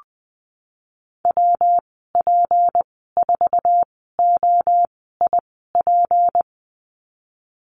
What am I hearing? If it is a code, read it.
Morse code, heard as a keyed tone, about 20 wpm, spaced as for 15 wpm, 700 Hz. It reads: WP4OIP
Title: 173 Lesson 28g - Number 9 - Call Signs 20wpm